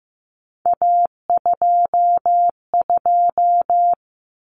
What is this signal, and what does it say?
Morse code, heard as a keyed tone, about 15 wpm, 700 Hz. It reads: A22
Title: Common Cut Numbers - 3-Digit 15wpm